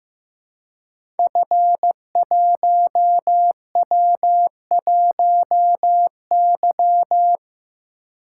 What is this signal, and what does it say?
Morse code, heard as a keyed tone, about 15 wpm, 700 Hz. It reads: F1W1Y
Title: Code Groups - 5 Letter:Numbers - 15wpm